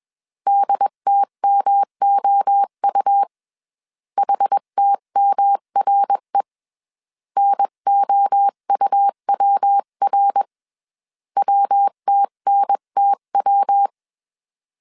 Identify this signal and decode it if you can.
Morse code, heard as a keyed tone, about 21 wpm, 800 Hz. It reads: DTMOU HTMRE NOUWR WTNTW
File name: 20wpm